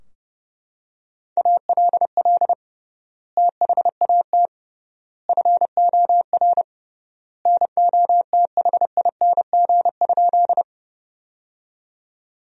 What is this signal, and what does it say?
Morse code, heard as a keyed tone, about 32 wpm, 700 Hz. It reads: ALL THAT FOR NOTHING?
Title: 3x Word-Spacing - Sentences from Top 500 Words - 30wpm